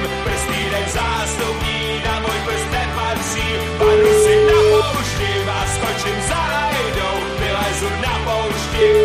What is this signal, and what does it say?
Ringback tone (Europe, 425 Hz), with noise.